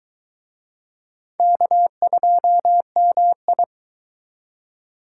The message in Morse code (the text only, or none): K2MI